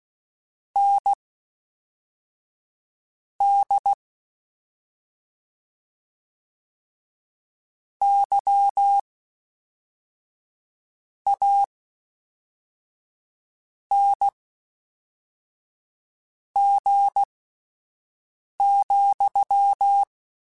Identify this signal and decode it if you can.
Morse code, heard as a keyed tone, about 16 wpm, spaced as for 4 wpm, 800 Hz. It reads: ND YANG,
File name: Misc / Misc-mp3s / WordPairs sp @5